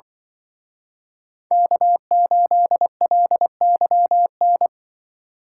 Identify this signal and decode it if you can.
Morse code, heard as a keyed tone, about 25 wpm, 700 Hz. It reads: K8LYN
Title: Speed Racing - USA 1x3 Call Signs 20wpm